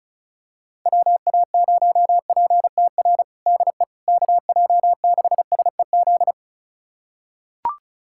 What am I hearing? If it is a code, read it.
Morse code, heard as a keyed tone, about 34 wpm, 700 Hz. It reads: WA0PTR DE KJ6SEZ
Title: USA Callsign DE Callsign - 6 Characters - 35wpm